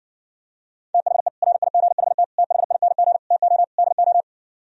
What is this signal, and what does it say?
Morse code, heard as a keyed tone, about 66 wpm, 700 Hz. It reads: THE RIGHT THING TO DO